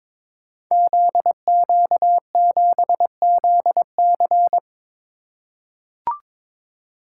Morse code, heard as a keyed tone, about 22 wpm, 700 Hz. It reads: ZQ7ZC
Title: Code Groups - 5 Letter:Numbers - 22wpm